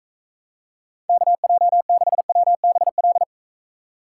Morse code, heard as a keyed tone, about 42 wpm, 700 Hz. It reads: KJ6WBL